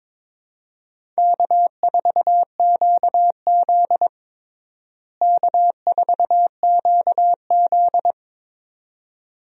Morse code, heard as a keyed tone, about 22 wpm, 700 Hz. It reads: K4QZ K4QZ
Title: USA Call Signs - 4 Characters - Repeated 2x - 22wpm